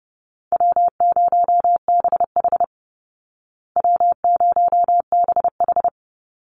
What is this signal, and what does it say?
Morse code, heard as a keyed tone, about 30 wpm, 700 Hz. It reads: W0BH W0BH
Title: Easy Call Signs Repeated 2x - 30wpm